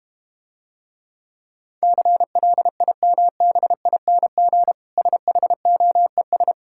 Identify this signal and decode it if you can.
Morse code, heard as a keyed tone, about 32 wpm, 700 Hz. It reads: CLIMBING SHOES